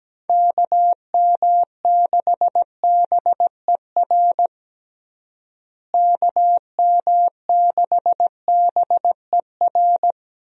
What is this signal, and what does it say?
Morse code, heard as a keyed tone, about 17 wpm, 700 Hz. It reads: KM6BER KM6BER
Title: USA Call Signs - 6 Characters - Repeated 3x - 17wpm